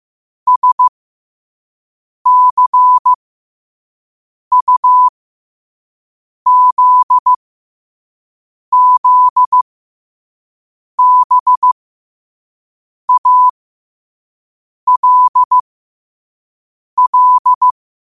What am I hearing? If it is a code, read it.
Morse code, heard as a keyed tone, about 15 wpm, spaced as for 5 wpm, 1000 Hz. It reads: SCUZZBALL
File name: qotdfiles / 5wpm